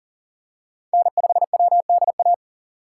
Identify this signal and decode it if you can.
Morse code, heard as a keyed tone, about 37 wpm, 700 Hz. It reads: N5WDA
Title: USA Call Signs - 5 Characters - 4x Character Spacing - 40wpm